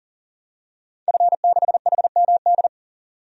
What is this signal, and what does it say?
Morse code, heard as a keyed tone, about 37 wpm, 700 Hz. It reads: F6HMD
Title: DXCC-100 Call Signs - 5 Characters - 40wpm